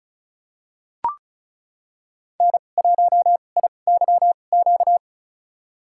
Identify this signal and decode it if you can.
Morse code, heard as a keyed tone, about 34 wpm, spaced as for 25 wpm, 700 Hz. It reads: N1IYQ
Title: USA Call Signs - 5 Characters - 2x Character Spacing - 35wpm